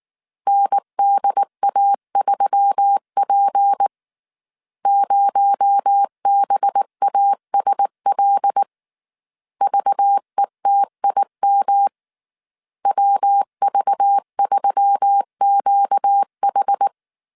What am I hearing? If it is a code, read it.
Morse code, heard as a keyed tone, about 19 wpm, 800 Hz. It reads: NDA3P 0BASL VETIM WV3QH